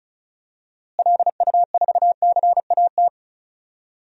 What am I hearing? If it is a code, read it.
Morse code, heard as a keyed tone, about 34 wpm, 700 Hz. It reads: LU4CAT
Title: DXCC-100 Call Signs - 6 Characters - Repeated 2x - 35wpm